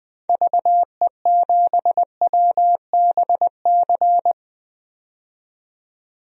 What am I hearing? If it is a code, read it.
Morse code, heard as a keyed tone, about 20 wpm, 700 Hz. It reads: VE7WBC
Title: DXCC-100 Call Signs - 6 Characters - 20wpm